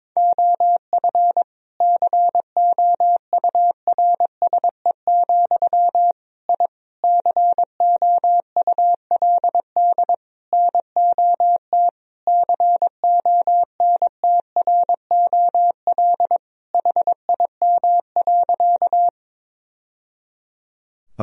Morse code, heard as a keyed tone, about 22 wpm, 700 Hz. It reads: OF COURSE, I COULD NOT CONTROL HIM.
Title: Sentences from Top 1000 Words - Review Part 3 - 22wpm